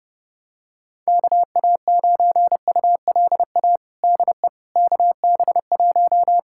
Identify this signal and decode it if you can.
Morse code, heard as a keyed tone, about 30 wpm, 700 Hz. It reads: KA9ULA DE KB1
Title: USA Callsign DE Callsign - 6 Characters - 30wpm